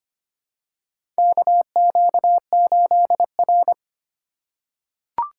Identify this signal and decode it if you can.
Morse code, heard as a keyed tone, about 25 wpm, 700 Hz. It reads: KQ8R
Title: USA Call Signs - 4x Character Spacing - 25wpm